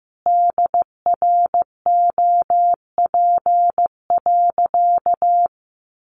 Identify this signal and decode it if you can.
Morse code, heard as a keyed tone, about 15 wpm, 700 Hz. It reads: DROP.
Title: Sentences from Top 1400 Words 15wpm